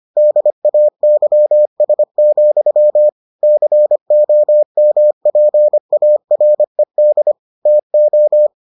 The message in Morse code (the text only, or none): DAYS, COMPARED TO